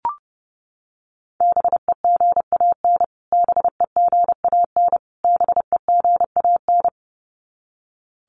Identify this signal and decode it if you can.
Morse code, heard as a keyed tone, about 30 wpm, 700 Hz. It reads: BEGAN BEGAN BEGAN
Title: Top 300 Words Repeated 30wpm